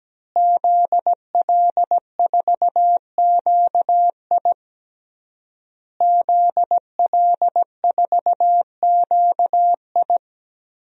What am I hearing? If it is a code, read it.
Morse code, heard as a keyed tone, about 17 wpm, 700 Hz. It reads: ZL4QI ZL4QI
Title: DXCC-100 Call Signs - 5 Characters - Repeated 3x - 17wpm